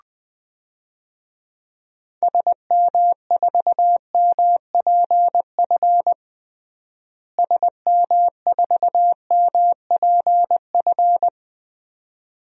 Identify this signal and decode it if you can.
Morse code, heard as a keyed tone, about 20 wpm, 700 Hz. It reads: SM4MPF SM4MPF
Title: DXCC-100 Call Signs - 6 Characters - Repeated 2x - 20wpm